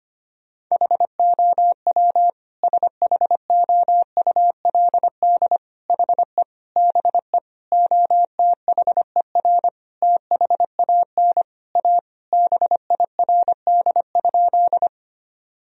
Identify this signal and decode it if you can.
Morse code, heard as a keyed tone, about 25 wpm, 700 Hz. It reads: HOW SHOULD HE BE OTHER THAN A BIRD?
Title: Sentences from Top 1000 Words - Review Part 2 - 25wpm